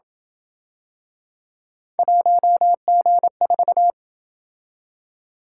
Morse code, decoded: 1G4

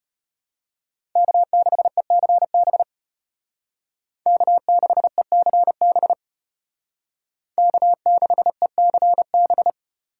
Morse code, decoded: K6ECB K6ECB K6ECB

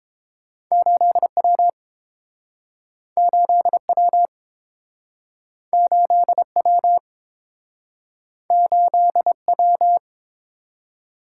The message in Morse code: 8W 8W 8W 8W

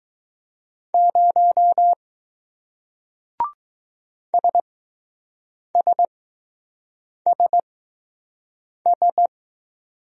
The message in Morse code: 0 SSSS

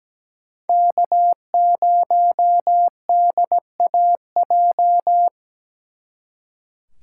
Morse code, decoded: K0DAJ

